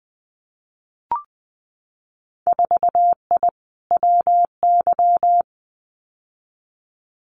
Morse code, heard as a keyed tone, about 20 wpm, 700 Hz. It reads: 4I WY
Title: Winter Field Day Exchanges Encore 20wpm